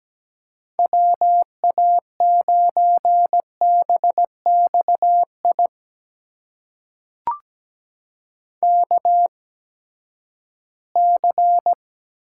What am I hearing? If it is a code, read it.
Morse code, heard as a keyed tone, about 17 wpm, spaced as for 12 wpm, 700 Hz. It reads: WA9BXI KC
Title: USA Call Signs - 6 Characters - 8x Character Spacing - 17wpm